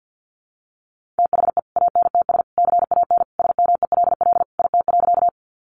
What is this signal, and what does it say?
Morse code, heard as a keyed tone, about 49 wpm, 700 Hz. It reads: THE ANTS CAN SMELL IT.
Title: Additional Easy Phrases 50wpm